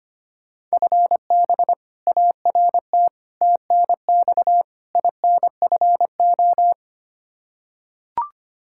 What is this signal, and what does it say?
Morse code, heard as a keyed tone, about 25 wpm, 700 Hz. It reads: FB ART TNX INFO
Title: K1USN SST QSO Practice 25wpm